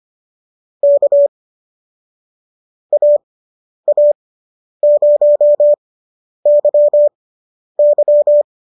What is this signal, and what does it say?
Morse code, heard as a keyed tone, about 25 wpm, spaced as for 10 wpm, 590 Hz. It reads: K AA0YY